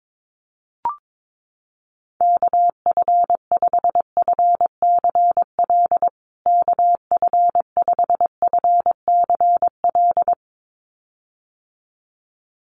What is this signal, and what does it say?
Morse code, heard as a keyed tone, about 22 wpm, 700 Hz. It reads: KF5FCL KF5FCL
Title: USA Call Signs Repeated 2x 22wpm